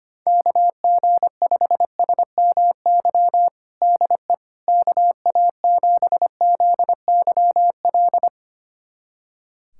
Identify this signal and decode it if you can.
Morse code, heard as a keyed tone, about 25 wpm, 700 Hz. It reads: KG5SMY DE KA7ZYL
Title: USA Callsign DE Callsign - 6 Characters - 25wpm